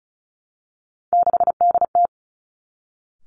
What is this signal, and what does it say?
Morse code, heard as a keyed tone, about 36 wpm, 700 Hz. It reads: 6DT
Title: Three Challenging Letter-Number Combinations 35wpm